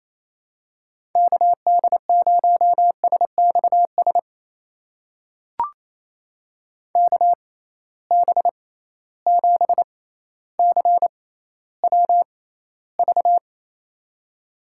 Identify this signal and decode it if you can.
Morse code, heard as a keyed tone, about 28 wpm, spaced as for 14 wpm, 700 Hz. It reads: KD0SXS KB7CWV
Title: USA Call Signs - 6 Characters - 6x Character Spacing - 28wpm